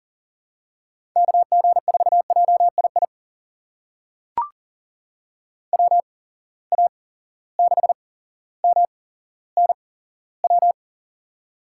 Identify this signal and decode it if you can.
Morse code, heard as a keyed tone, about 39 wpm, spaced as for 17 wpm, 700 Hz. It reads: KG4JII WA6MNW
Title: USA Call Signs - 6 Characters - 8x Character Spacing - 40wpm